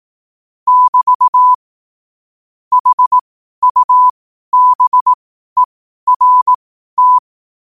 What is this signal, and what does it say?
Morse code, heard as a keyed tone, about 18 wpm, spaced as for 12 wpm, 1000 Hz. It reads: <BT> HUBERT
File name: qotdfiles / 12wpm